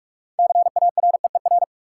Morse code, heard as a keyed tone, about 43 wpm, 700 Hz. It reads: CAREER
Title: Sentences from Top 1000 Words - New Word Drill - 45wpm